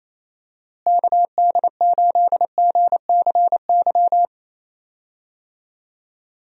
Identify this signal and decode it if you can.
Morse code, heard as a keyed tone, about 27 wpm, 700 Hz. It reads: KD8GCY
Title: USA Call Signs - 6 Characters - Repeated 3x - 28wpm